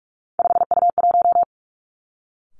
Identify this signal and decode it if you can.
Morse code, heard as a keyed tone, about 46 wpm, 700 Hz. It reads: 5U1
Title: Three Challenging Letter-Number Combinations 45wpm